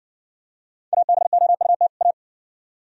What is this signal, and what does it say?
Morse code, heard as a keyed tone, about 56 wpm, 690 Hz. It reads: ABOUT A